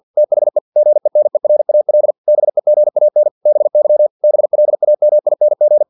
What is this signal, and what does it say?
Morse code, heard as a keyed tone, about 50 wpm, 600 Hz. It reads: THE GENERAL BEGAN BY BLAMING
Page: Never 49wpm